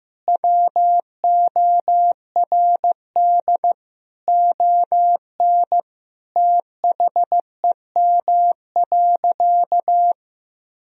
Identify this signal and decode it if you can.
Morse code, heard as a keyed tone, about 16 wpm, 700 Hz. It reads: WORD ON THEM.
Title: Sentences from Top 1000 Words - Review Part 4 - 15wpm